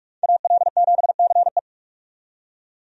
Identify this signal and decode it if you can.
Morse code, heard as a keyed tone, about 46 wpm, 700 Hz. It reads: AL7CE
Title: USA Call Signs - 5 Characters - 45wpm